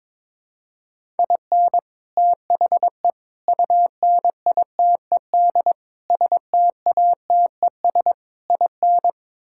Morse code, heard as a keyed tone, about 22 wpm, 700 Hz. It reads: IN THE UNITED STATES IN
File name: practice / 22wpm / Build A Phrase - 5-Word Phrases - 22wpm